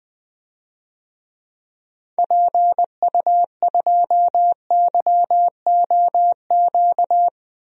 Morse code, heard as a keyed tone, about 20 wpm, 700 Hz. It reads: PU2YOQ
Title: DXCC-100 Call Signs - 6 Characters - 20wpm